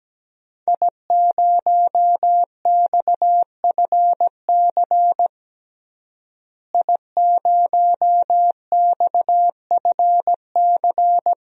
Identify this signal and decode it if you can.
Morse code, heard as a keyed tone, about 17 wpm, 700 Hz. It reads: I0XFC I0XFC
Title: DXCC-100 Call Signs - 5 Characters - Repeated 3x - 17wpm